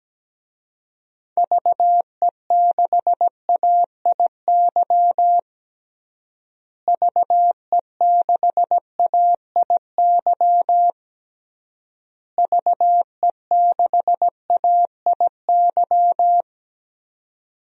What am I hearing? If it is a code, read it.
Morse code, heard as a keyed tone, about 17 wpm, 700 Hz. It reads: VE6AIY VE6AIY VE6AIY
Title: DXCC-100 Call Signs - 6 Characters - Repeated 3x - 17wpm